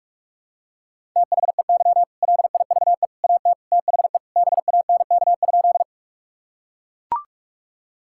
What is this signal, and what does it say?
Morse code, heard as a keyed tone, about 44 wpm, 700 Hz. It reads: THEY LIVE AT THE BANK?
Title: Sentences from Top 1000 Words - Review Part 6 - 45wpm